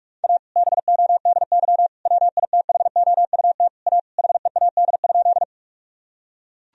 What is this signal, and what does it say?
Morse code, heard as a keyed tone, about 41 wpm, 700 Hz. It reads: A BODY WITHOUT A HEAD?
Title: Sentences from Top 1000 Words - Review Part 3 - 45wpm